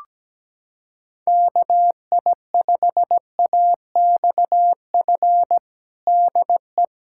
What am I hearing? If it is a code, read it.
Morse code, heard as a keyed tone, about 17 wpm, 700 Hz. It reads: KI5AXF DE
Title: USA Callsign DE Callsign - 6 Characters - 17wpm